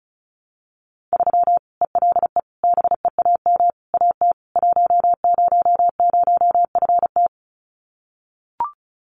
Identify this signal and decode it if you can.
Morse code, heard as a keyed tone, about 34 wpm, 700 Hz. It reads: 3 ELE BEAM AT 100FT
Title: Common QSO Phrases 35wpm